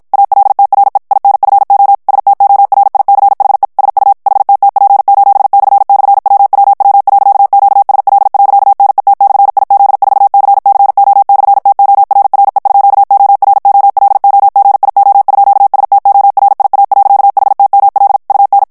Morse code, heard as a keyed tone, about 56 wpm, 800 Hz. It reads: ALTRE INFO STORICHE SU HTTP://WWW.QSL.NET/IZ4CCO/TORRE_OROLOGIO_STORIA.HTML UN
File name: morse mp3 / ITALIEN texte clair 1 vitesse 50 mots minute